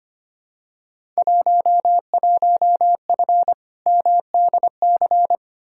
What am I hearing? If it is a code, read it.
Morse code, heard as a keyed tone, about 25 wpm, 700 Hz. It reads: 11F MDC
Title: Field Day Exchanges Encore - 25wpm